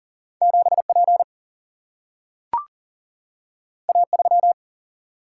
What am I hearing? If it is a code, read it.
Morse code, heard as a keyed tone, about 39 wpm, 700 Hz. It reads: 7P A3
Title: Single-Letter Single-Number 40wpm